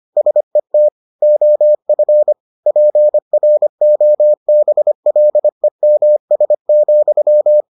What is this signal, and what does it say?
Morse code, heard as a keyed tone, about 25 wpm, 600 Hz. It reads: SET OF PROBLEMS,